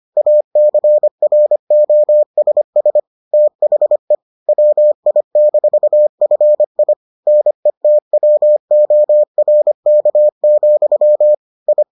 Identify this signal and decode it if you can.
Morse code, heard as a keyed tone, about 25 wpm, 600 Hz. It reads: ACROSS THE WI-FI NETWORK, I